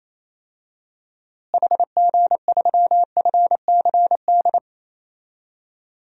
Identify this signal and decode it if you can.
Morse code, heard as a keyed tone, about 28 wpm, 700 Hz. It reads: HG3FCD
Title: DXCC-100 Call Signs - 6 Characters - Repeated 2x - 28wpm